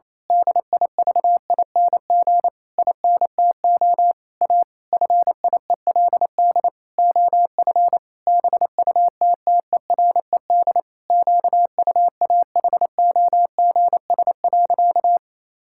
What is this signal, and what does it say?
Morse code, heard as a keyed tone, about 27 wpm, 700 Hz. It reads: DIVING INTO A FIELD OF BUTTERED QUAHOGS.
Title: Pangram Challenge v2 - 28wpm